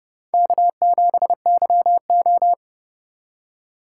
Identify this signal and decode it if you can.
Morse code, heard as a keyed tone, about 28 wpm, 700 Hz. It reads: K7YO